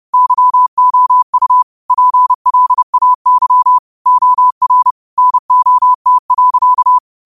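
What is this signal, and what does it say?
Morse code, heard as a keyed tone, about 30 wpm, 1000 Hz. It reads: YOU PLAY OR NOT.
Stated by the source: Morse code quote of the day sent at 30 wpm.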